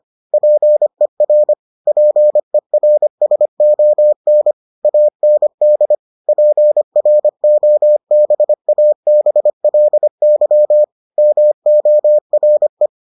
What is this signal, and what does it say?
Morse code, heard as a keyed tone, about 25 wpm, 600 Hz. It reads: PER PERSON AND PROBABLY MORE